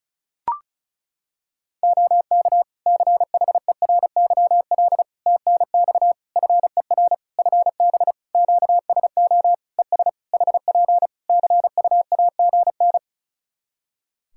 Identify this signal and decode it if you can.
Morse code, heard as a keyed tone, about 34 wpm, 700 Hz. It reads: OK CHERYL TNX FER FB QSO ES HP CUAGN
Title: Practice QSO - Exchange 4 - Ending - 35wpm